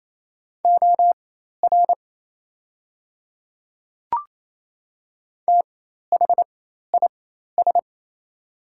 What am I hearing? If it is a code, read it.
Morse code, heard as a keyed tone, about 27 wpm, spaced as for 13 wpm, 700 Hz. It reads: OR THIS
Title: 4x Character Spacing - Top 100 Words - 28wpm